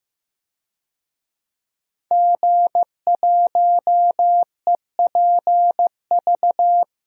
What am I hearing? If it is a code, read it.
Morse code, heard as a keyed tone, about 15 wpm, 700 Hz. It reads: G1EPV
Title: DXCC-100 Call Signs - 5 Characters - Repeated 2x - 15wpm